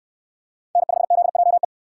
Morse code, heard as a keyed tone, about 61 wpm, 700 Hz. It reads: N5ZJE